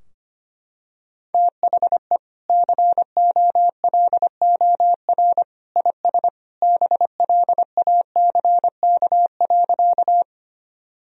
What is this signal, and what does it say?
Morse code, heard as a keyed tone, about 25 wpm, 700 Hz. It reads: THE COLOR IS BLACK.
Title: Sentences from Top 1000 Words - Review Part 2 - 25wpm